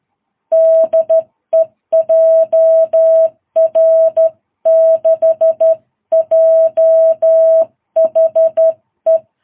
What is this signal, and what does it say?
Morse code, heard as a keyed tone, about 12 wpm, 660 Hz. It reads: DE JR6JHE
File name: jh8cbh / 2022cw